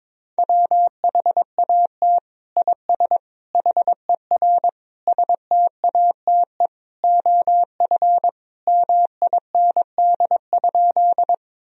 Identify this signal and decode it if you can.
Morse code, heard as a keyed tone, about 22 wpm, 700 Hz. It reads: WHAT IS HER STATE OF MIND?
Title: Sentences from Top 1000 Words - Review Part 2 - 22wpm